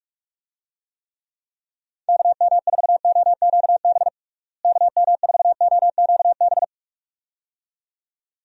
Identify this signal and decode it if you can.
Morse code, heard as a keyed tone, about 46 wpm, 700 Hz. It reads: KM4OQB KM4OQB
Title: USA Call Signs - 6 Characters - Repeated 2x - 45wpm